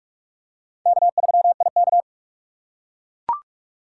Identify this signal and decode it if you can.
Morse code, heard as a keyed tone, about 43 wpm, 700 Hz. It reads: K3IK
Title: USA Call Signs - 4 Characters - 4x Character Spacing - 45wpm